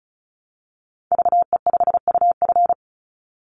Morse code, heard as a keyed tone, about 34 wpm, 700 Hz. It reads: VE5UF
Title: Call Signs Repeated 3x - 35wpm